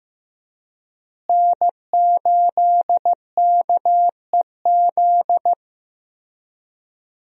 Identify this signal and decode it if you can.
Morse code, heard as a keyed tone, about 15 wpm, 700 Hz. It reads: N8KEZ
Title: USA Call Signs - 5 Characters - Repeated 2x - 15wpm